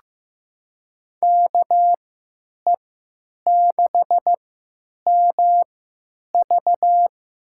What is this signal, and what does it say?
Morse code, heard as a keyed tone, about 15 wpm, spaced as for 9 wpm, 700 Hz. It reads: KE6MV
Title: USA Call Signs - 3x Character Spacing - 15wpm